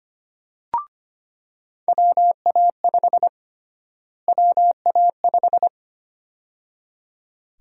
Morse code, heard as a keyed tone, about 25 wpm, 700 Hz. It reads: WA5 WA5
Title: USA Call Sign Prefixes - Repeated 2x - 25wpm